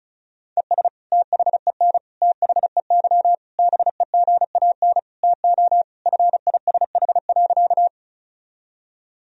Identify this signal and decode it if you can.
Morse code, heard as a keyed tone, about 36 wpm, 700 Hz. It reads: ES THEN THEY BEGAN TO FISH.
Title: Sentences from Top 300 Words and Abbreviations 35wpm